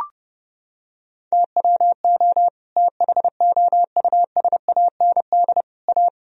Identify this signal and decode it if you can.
Morse code, heard as a keyed tone, about 32 wpm, 700 Hz. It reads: TWO THOUSAND A
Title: Sentences from Top 1000 Words - Review Part 3 - 30wpm